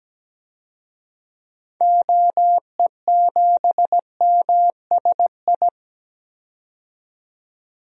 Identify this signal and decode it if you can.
Morse code, heard as a keyed tone, about 17 wpm, 700 Hz. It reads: OE7MSI
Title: DXCC-100 Call Signs - 6 Characters - 17wpm